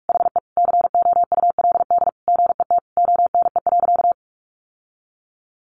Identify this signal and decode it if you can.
Morse code, heard as a keyed tone, about 44 wpm, 700 Hz. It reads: HE COULD GET ONE.
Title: Sentences from Top 1400 Words 45wpm